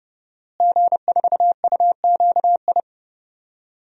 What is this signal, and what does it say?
Morse code, heard as a keyed tone, about 30 wpm, 700 Hz. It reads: G4UQI
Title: DXCC-100 Call Signs - 5 Characters - Repeated 2x - 30wpm